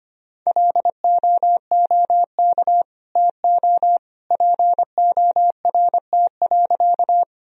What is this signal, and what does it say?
Morse code, heard as a keyed tone, about 25 wpm, 700 Hz. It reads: LOOK TO PORT.
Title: Sentences from Top 1000 Words - Review Part 1 - 25wpm